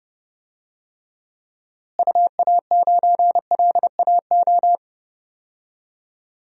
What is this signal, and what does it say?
Morse code, heard as a keyed tone, about 30 wpm, 700 Hz. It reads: UA9LAO
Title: DXCC-100 Call Signs - 6 Characters - Repeated 2x - 30wpm